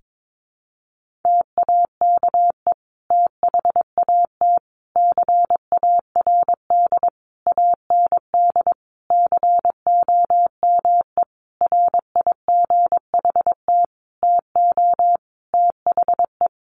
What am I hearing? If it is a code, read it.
Morse code, heard as a keyed tone, about 22 wpm, 700 Hz. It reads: TAKE THAT CARD AND COME RIGHT TO THE
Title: Sentences from Top 1400 Words 22wpm